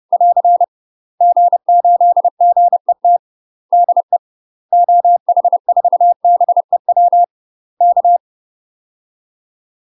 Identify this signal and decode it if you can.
Morse code, heard as a keyed tone, about 30 wpm, 700 Hz. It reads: <AR> G8GET DE OH4BEW K
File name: UK MASTER QSOs TRAINING/30WPM